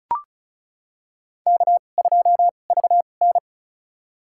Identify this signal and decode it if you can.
Morse code, heard as a keyed tone, about 36 wpm, spaced as for 25 wpm, 690 Hz. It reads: K2VN
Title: USA Call Signs - 4 Characters - 2x Character Spacing - 35wpm